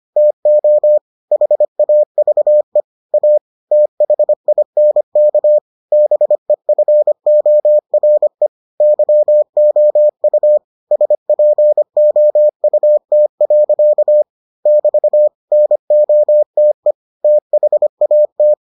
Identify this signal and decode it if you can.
Morse code, heard as a keyed tone, about 25 wpm, 600 Hz. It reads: TO HAVE A THINK BEFORE YOU SPOUT. <BT> NOTE THAT